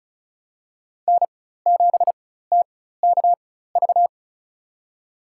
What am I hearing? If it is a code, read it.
Morse code, heard as a keyed tone, about 34 wpm, spaced as for 16 wpm, 700 Hz. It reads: N7TKV
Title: USA Call Signs - 5 Characters - 4x Character Spacing - 35wpm